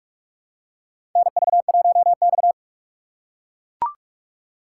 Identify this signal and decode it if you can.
Morse code, heard as a keyed tone, about 43 wpm, 700 Hz. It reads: NV1X
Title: USA Call Signs - 4 Characters - 4x Character Spacing - 45wpm